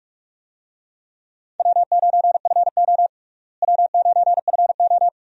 Morse code, heard as a keyed tone, about 46 wpm, 700 Hz. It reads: W9FO W9FO
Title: USA Call Signs - 4 Characters - Repeated 3x - 45wpm